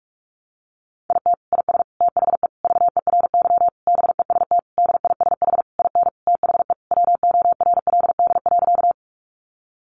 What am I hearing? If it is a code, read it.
Morse code, heard as a keyed tone, about 44 wpm, 700 Hz. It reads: IT IS THE VERY BEST DISH IN THE WORLD.
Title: Sentences from Top 1300 Words 45wpm